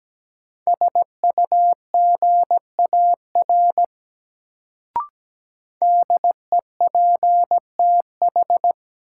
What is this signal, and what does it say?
Morse code, heard as a keyed tone, about 17 wpm, 700 Hz. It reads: SUGAR DEPTH
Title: Sentences from Top 1200 Words - New Word Drill Alt - 17wpm